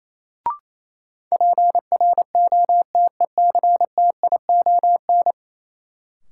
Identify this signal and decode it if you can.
Morse code, heard as a keyed tone, about 28 wpm, 700 Hz. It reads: PROTECTION
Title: Sentences from Top 1600 Words - New Word Drill Alt - 28wpm